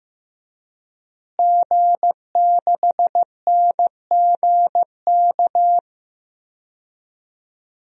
Morse code, heard as a keyed tone, about 15 wpm, 700 Hz. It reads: G6NGK
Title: DXCC-100 Call Signs - 5 Characters - Repeated 3x - 15wpm